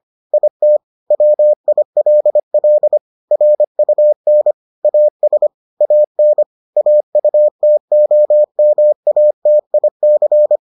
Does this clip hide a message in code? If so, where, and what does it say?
Yes: Morse code from 0.3 to 10.6 s, heard as a keyed tone, about 25 wpm, 600 Hz. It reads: IT WILL RUN AS AN AUTOMATIC